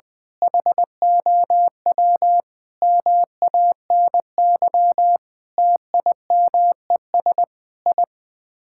Morse code, heard as a keyed tone, about 21 wpm, 700 Hz. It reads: HOW MANY TIMES I
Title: Sets of 8 Words - 20wpm